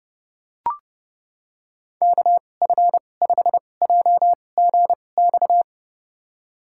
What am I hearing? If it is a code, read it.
Morse code, heard as a keyed tone, about 30 wpm, spaced as for 22 wpm, 700 Hz. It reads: KF5JGX
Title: USA Call Signs - 6 Characters - 2x Character Spacing - 30wpm